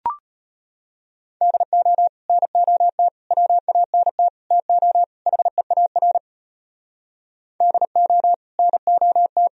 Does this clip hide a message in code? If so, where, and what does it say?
Yes: Morse code from 1.4 to 9.5 s, heard as a keyed tone, about 32 wpm, 700 Hz. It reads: DO NOT WANT TO HEAR DO NOT